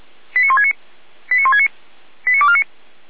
Five-tone selective call (ZVEI), heard as quick runs of five tones, tones of about 70 ms. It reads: 89169 89169 89249